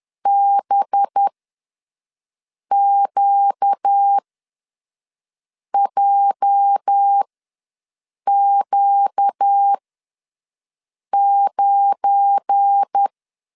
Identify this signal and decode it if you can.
Morse code, heard as a keyed tone, about 10 wpm, spaced as for 5 wpm, 800 Hz. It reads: BQJQ9
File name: LEZIONI / 10 SIGNS / 10B INT ws 10wpm 4134